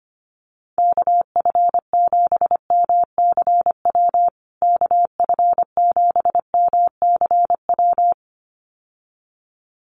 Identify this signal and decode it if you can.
Morse code, heard as a keyed tone, about 25 wpm, 700 Hz. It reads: KF7MCW KF7MCW
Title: USA Call Signs Repeated 2x 25wpm